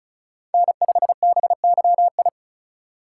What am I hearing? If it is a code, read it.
Morse code, heard as a keyed tone, about 36 wpm, 700 Hz. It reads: N5BYI
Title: USA Call Signs - 5 Characters - 4x Character Spacing - 35wpm